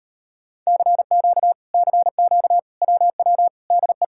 Morse code, heard as a keyed tone, about 38 wpm, 700 Hz. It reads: CQ CQ WW DE